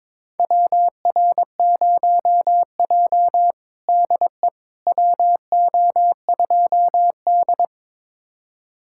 Morse code, heard as a keyed tone, about 22 wpm, 700 Hz. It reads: WR0J DE WO2D
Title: USA Callsign DE Callsign - 4 Characters - 22wpm